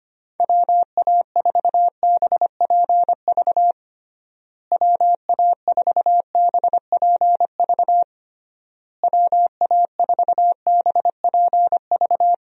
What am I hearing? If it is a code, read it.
Morse code, heard as a keyed tone, about 25 wpm, 700 Hz. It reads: WA4BPV WA4BPV WA4BPV
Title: USA Call Signs - 6 Characters - Repeated 3x - 25wpm